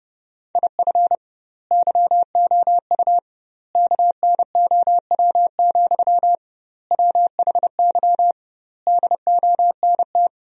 Morse code, heard as a keyed tone, about 32 wpm, 700 Hz. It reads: IF YOU KNOW, WHY DONT
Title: 2x Word-Spacing - Sentences from Top 300 Words - 30wpm